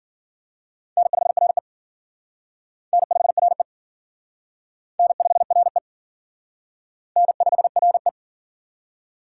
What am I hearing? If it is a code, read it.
Morse code, heard as a keyed tone, about 51 wpm, 700 Hz. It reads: N5RE N5RE N5RE N5RE